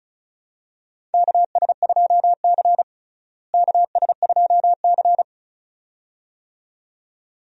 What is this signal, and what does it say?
Morse code, heard as a keyed tone, about 36 wpm, 700 Hz. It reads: KS2C KS2C
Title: USA Call Signs - 4 Characters - Repeated 2x - 35wpm